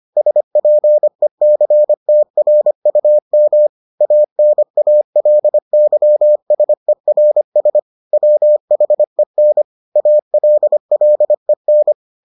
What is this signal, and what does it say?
Morse code, heard as a keyed tone, about 25 wpm, 600 Hz. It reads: SPECTRUM ANALYSERS WHEN ALLEN